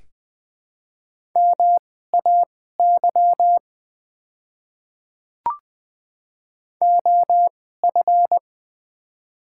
Morse code, heard as a keyed tone, about 21 wpm, spaced as for 15 wpm, 700 Hz. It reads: MAY OF
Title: 2x Character Spacing - Top 100 Words - 20wpm